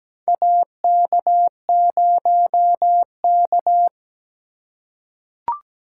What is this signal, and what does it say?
Morse code, heard as a keyed tone, about 17 wpm, 700 Hz. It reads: AK0K
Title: USA Call Signs - 4 Characters - 3x Character Spacing - 17wpm